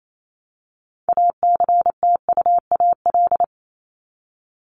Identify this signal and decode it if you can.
Morse code, heard as a keyed tone, about 28 wpm, 700 Hz. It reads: ACTUAL
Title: LY Suffix - Top 100 Words with Base Words 28wpm